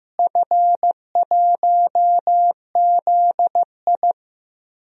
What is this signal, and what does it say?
Morse code, heard as a keyed tone, about 15 wpm, 700 Hz. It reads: F1ZI
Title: DXCC-100 Call Signs - 4 Characters - Repeated 3x - 15wpm